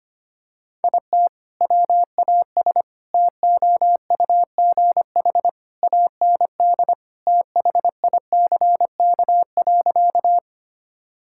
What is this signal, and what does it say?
Morse code, heard as a keyed tone, about 25 wpm, 700 Hz. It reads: IT WAS TOUGH AND THICK.
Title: Sentences from Top 1600 Words 25wpm